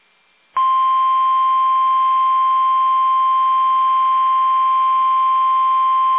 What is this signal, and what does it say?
Emergency Alert System (SAME) tones (the 1050 Hz attention tone). It is heard from 0.5 to 5.6 s.
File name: eas / KUSO / KUSO